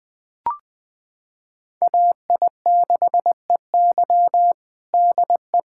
Morse code, heard as a keyed tone, about 21 wpm, 700 Hz. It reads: AI6EY DE
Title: USA Callsign DE Callsign - 5 Characters - 20wpm